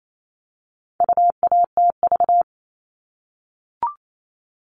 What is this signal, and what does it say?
Morse code, heard as a keyed tone, about 28 wpm, 700 Hz. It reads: UATV
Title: All Cut Numbers - 4-Digit 28wpm